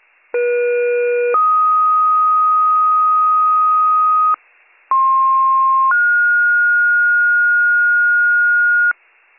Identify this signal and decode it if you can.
Two-tone page, heard as a one-second tone, then a three-second tone, about 480 then 1250 Hz, about 1060 then 1430 Hz, with noise. It is heard from 0.3 to 8.9 s.